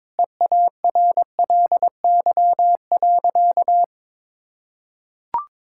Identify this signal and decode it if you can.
Morse code, heard as a keyed tone, about 22 wpm, 700 Hz. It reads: EARLY.
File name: Sentences with 7-Word Phrases - 22wpm